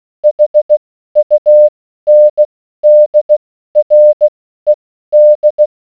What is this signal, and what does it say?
Morse code, heard as a keyed tone, about 16 wpm, spaced as for 12 wpm, 600 Hz. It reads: HUNDRED